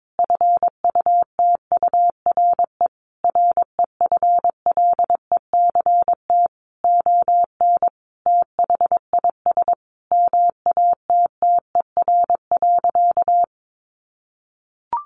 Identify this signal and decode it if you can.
Morse code, heard as a keyed tone, about 22 wpm, 700 Hz. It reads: FUTURE REFLECT ON THIS MATTER.
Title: Sentences from Top 1500 Words 22wpm